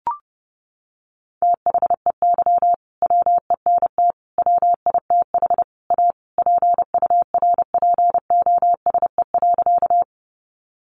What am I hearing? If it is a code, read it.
Morse code, heard as a keyed tone, about 30 wpm, 700 Hz. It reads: THEY WENT WITH A PURPOSE.
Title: Sentences from Top 800 Words 30wpm